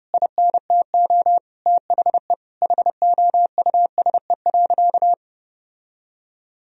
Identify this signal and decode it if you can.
Morse code, heard as a keyed tone, about 30 wpm, 700 Hz. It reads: INTO THE HOUSE.
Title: Sentences from Top 1000 Words - Review Part 1 - 30wpm